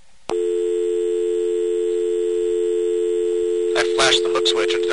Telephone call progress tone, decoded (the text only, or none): dial tone